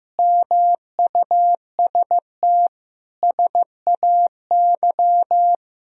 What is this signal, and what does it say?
Morse code, heard as a keyed tone, about 15 wpm, 700 Hz. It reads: MUST SAY
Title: Sentences from Top 500 Words and Abbreviations 15wpm